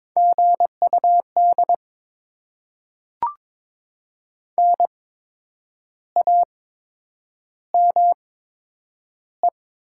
Morse code, heard as a keyed tone, about 22 wpm, spaced as for 8 wpm, 700 Hz. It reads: GUD NAME
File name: Top 20 QSO Elements - 8x Character Spacing - 22wpm